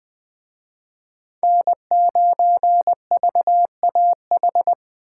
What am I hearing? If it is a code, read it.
Morse code, heard as a keyed tone, about 20 wpm, 700 Hz. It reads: N9VAH
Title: Speed Racing - USA Call Signs 17wpm